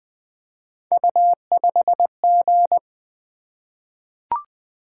Morse code, heard as a keyed tone, about 20 wpm, 700 Hz. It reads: U5G
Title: Three Letter-Number Combinations - 20wpm